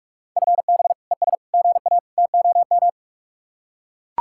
Morse code, heard as a keyed tone, about 44 wpm, 700 Hz. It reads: FB ES GA TOM